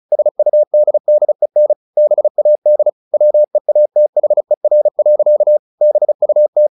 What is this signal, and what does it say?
Morse code, heard as a keyed tone, about 34 wpm, 600 Hz. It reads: SUDDEN BAD WEATHER. BUT
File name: the old man and the sea / the old man and the sea 1 35wpm